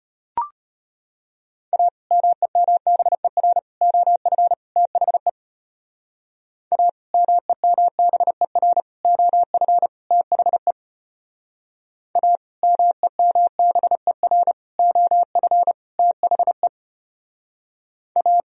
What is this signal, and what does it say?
Morse code, heard as a keyed tone, about 31 wpm, 700 Hz. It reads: A MEMBER OF THE A MEMBER OF THE A MEMBER OF THE A